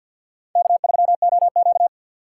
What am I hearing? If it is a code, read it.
Morse code, heard as a keyed tone, about 62 wpm, 700 Hz. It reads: K3OQ
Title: Speed Racing - USA 1x2 Call Signs 50wpm